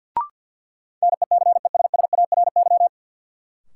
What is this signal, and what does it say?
Morse code, heard as a keyed tone, about 50 wpm, 700 Hz. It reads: NECESSARY
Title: Sentences from Top 800 Words - New Word Drill - 50wpm